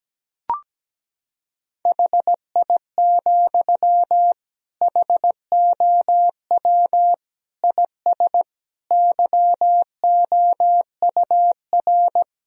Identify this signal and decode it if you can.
Morse code, heard as a keyed tone, about 17 wpm, 700 Hz. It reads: HI, HOW IS YOUR
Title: Sentences from Top 1000 Words - Review Part 3 - 17wpm